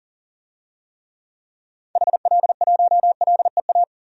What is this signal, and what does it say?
Morse code, heard as a keyed tone, about 40 wpm, 700 Hz. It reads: HL1LEA
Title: DXCC-100 Call Signs - 6 Characters - 40wpm